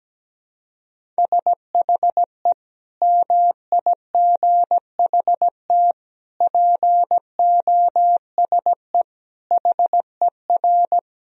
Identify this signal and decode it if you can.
Morse code, heard as a keyed tone, about 17 wpm, 700 Hz. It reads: SHE MIGHT POSE HER
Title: Sentences from Top 1000 Words - Review Part 2 - 17wpm